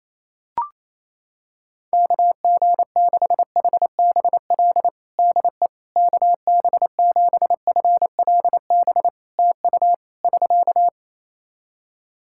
Morse code, heard as a keyed tone, about 28 wpm, 700 Hz. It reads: KG6HBL DE KB7FLB TU <SK>
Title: Practice QSO - Exchange 4 - Ending - 28wpm